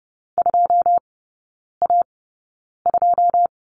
Morse code, heard as a keyed tone, about 30 wpm, spaced as for 9 wpm, 700 Hz. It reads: 2A2
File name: Lesson 18e - Letter m - Three Characters 30wpm